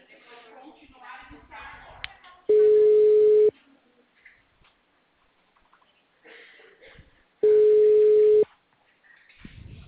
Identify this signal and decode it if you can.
Ringback tone (Europe, 425 Hz), with speech.
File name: out- -2012- - - .345